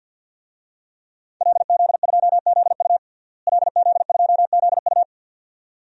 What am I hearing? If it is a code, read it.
Morse code, heard as a keyed tone, about 47 wpm, 700 Hz. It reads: LZ2ZU LZ2ZU
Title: DXCC-100 Call Signs - 5 Characters - Repeated 2x - 50wpm